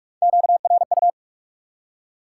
Morse code, heard as a keyed tone, about 46 wpm, 700 Hz. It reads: QRU
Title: Top 90 QSO Elements 45wpm